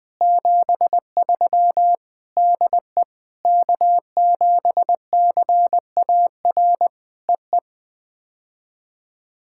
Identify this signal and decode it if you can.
Morse code, heard as a keyed tone, about 20 wpm, 700 Hz. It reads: 73 DE K7CAR EE
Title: POTA QSO Practice - 20wpm